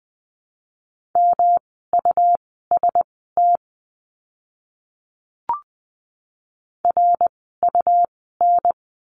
Morse code, heard as a keyed tone, about 20 wpm, spaced as for 14 wpm, 700 Hz. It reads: MUST RUN
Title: 237 Lesson 38f - Letter q - Words 20wpm